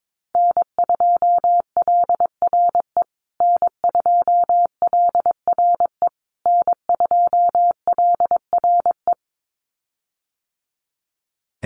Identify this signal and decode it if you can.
Morse code, heard as a keyed tone, about 22 wpm, 700 Hz. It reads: N2LRE N2LRE N2LRE
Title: USA Call Signs Repeated 3x 22wpm